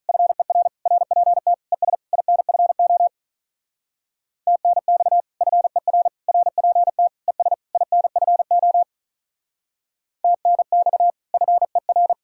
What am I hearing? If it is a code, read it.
Morse code, heard as a keyed tone, about 43 wpm, 700 Hz. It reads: FER RPT ES INFO TNX FER RPT ES INFO TNX FER
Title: Speed-Racing Common QSO Phrases - 35wpm